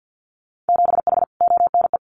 Morse code, heard as a keyed tone, about 50 wpm, 700 Hz. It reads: 7H ONE
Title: Winter Field Day Exchanges Encore 50wpm